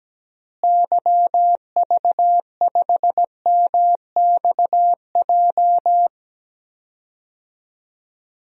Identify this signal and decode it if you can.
Morse code, heard as a keyed tone, about 17 wpm, 700 Hz. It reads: YV5MXJ
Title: DXCC-100 Call Signs - 6 Characters - Repeated 2x - 17wpm